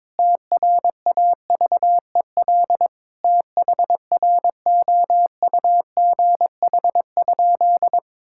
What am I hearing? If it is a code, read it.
Morse code, heard as a keyed tone, about 22 wpm, 700 Hz. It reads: TRAVEL THROUGH?
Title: Sentences from Top 1000 Words - Review Part 6 - 22wpm